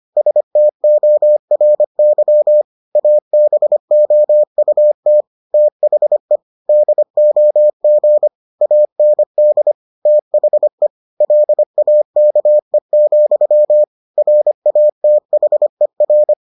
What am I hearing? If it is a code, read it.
Morse code, heard as a keyed tone, about 25 wpm, 600 Hz. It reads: STORY ABOUT THE DOG AND THE LAKE, RATHER